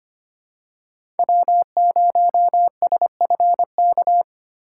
Morse code, heard as a keyed tone, about 25 wpm, 700 Hz. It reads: W0SFK
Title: USA Call Signs - 5 Characters - 8x Character Spacing - 25wpm